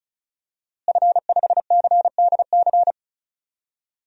Morse code, heard as a keyed tone, about 36 wpm, 700 Hz. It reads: F5CDC
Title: DXCC-100 Call Signs - 5 Characters - Repeated 2x - 35wpm